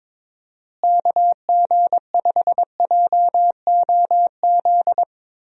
Morse code, heard as a keyed tone, about 21 wpm, 700 Hz. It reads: KG5JOZ
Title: USA Call Signs - 6 Characters - Repeated 2x - 22wpm